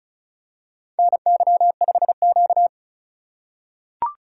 Morse code, heard as a keyed tone, about 36 wpm, 700 Hz. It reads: NY5Q
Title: USA Call Signs - 4 Characters - 2x Character Spacing - 35wpm